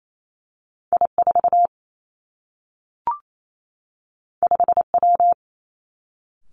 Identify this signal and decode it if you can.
Morse code, heard as a keyed tone, about 28 wpm, 700 Hz. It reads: I 4 5 W
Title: Two Challenging Letter-Number Combinations 28wpm